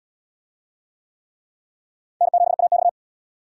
Morse code, heard as a keyed tone, about 76 wpm, 700 Hz. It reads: N7AX